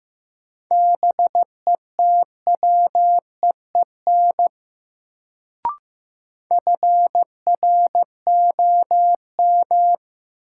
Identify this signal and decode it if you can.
Morse code, heard as a keyed tone, about 15 wpm, 700 Hz. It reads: BETWEEN FROM
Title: Sets of 2 Words - New Word Drill - 15wpm